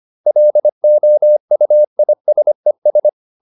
Morse code, heard as a keyed tone, about 25 wpm, 600 Hz. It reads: LOUISES
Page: Ferie 25 wpm